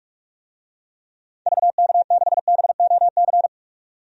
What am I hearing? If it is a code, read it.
Morse code, heard as a keyed tone, about 41 wpm, 700 Hz. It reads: VK6BOC
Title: DXCC-100 Call Signs - 6 Characters - 45wpm